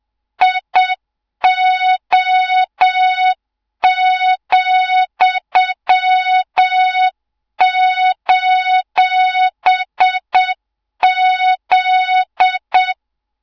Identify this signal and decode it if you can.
Morse code, heard as a keyed tone, about 7 wpm, 3700 Hz, with noise. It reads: IO,:Z
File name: Morze-HA2MN / hang / 24 GYAKORLAT